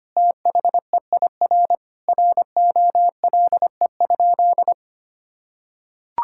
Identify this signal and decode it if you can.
Morse code, heard as a keyed tone, about 25 wpm, 700 Hz. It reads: THEIR ROLE?
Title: Sentences from Top 1000 Words - Review Part 6 - 25wpm